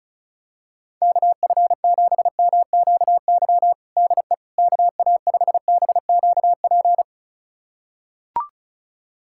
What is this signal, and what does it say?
Morse code, heard as a keyed tone, about 36 wpm, 700 Hz. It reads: KF7MQY DE KA5BQP
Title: USA Callsign DE Callsign - 6 Characters - 35wpm